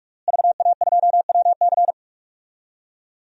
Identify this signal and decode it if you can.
Morse code, heard as a keyed tone, about 47 wpm, 700 Hz. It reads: VA2WC
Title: DXCC-100 Call Signs - 5 Characters - Repeated 3x - 45wpm